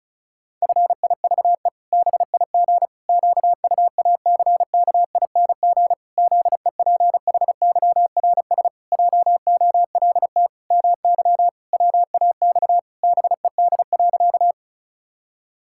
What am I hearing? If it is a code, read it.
Morse code, heard as a keyed tone, about 36 wpm, 700 Hz. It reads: FIVE BIG QUACKING ZEPHYRS JOLT MY WAX BED.
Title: Pangram Challenge v2 - 35wpm